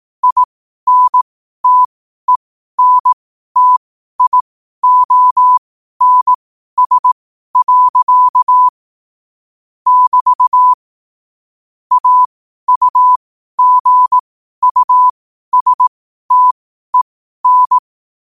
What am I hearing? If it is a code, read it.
Morse code, heard as a keyed tone, about 18 wpm, spaced as for 12 wpm, 1000 Hz. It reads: INTENTIONS. <BT> AUGUSTEN
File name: qotdfiles / 12wpm